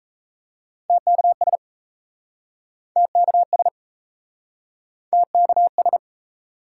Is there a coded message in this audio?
Yes: Morse code from 0.9 to 6.0 s, heard as a keyed tone, about 38 wpm, 700 Hz. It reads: TKS TKS TKS